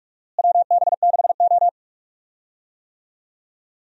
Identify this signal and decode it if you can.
Morse code, heard as a keyed tone, about 46 wpm, 700 Hz. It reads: WB6O
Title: USA Call Signs - 4 Characters - Repeated 2x - 45wpm